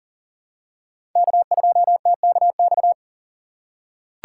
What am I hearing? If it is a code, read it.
Morse code, heard as a keyed tone, about 37 wpm, 700 Hz. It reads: K2TKX
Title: USA Call Signs - 5 Characters - 3x Character Spacing - 40wpm